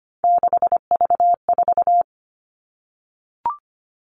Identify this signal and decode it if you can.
Morse code, heard as a keyed tone, about 25 wpm, 700 Hz. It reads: 6V4
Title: All Cut Numbers - 3-Digit 25wpm